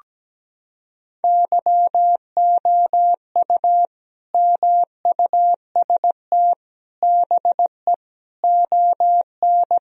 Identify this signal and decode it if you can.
Morse code, heard as a keyed tone, about 17 wpm, 700 Hz. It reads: YOU MUST BE ON
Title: Sentences from Top 1000 Words - Review Part 4 - 17wpm